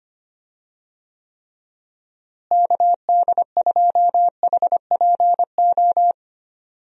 Morse code, heard as a keyed tone, about 25 wpm, 700 Hz. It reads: KD2HPO
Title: USA Call Signs - 6 Characters - Repeated 2x - 25wpm